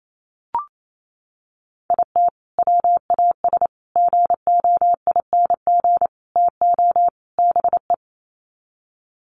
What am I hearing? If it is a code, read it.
Morse code, heard as a keyed tone, about 28 wpm, 700 Hz. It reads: IT WAS GOING TO BE
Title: Sets of 5 Words 28wpm